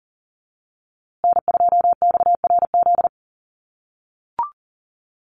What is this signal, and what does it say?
Morse code, heard as a keyed tone, about 40 wpm, 700 Hz. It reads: N2XRZ
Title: USA Call Signs Repeated 3x 40wpm